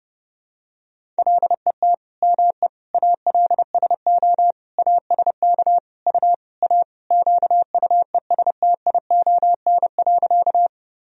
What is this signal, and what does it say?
Morse code, heard as a keyed tone, about 30 wpm, 700 Hz. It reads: LET ME ALSO ASK U A QUESTION.
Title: Sentences from Top 400 Words and Abbreviations 30wpm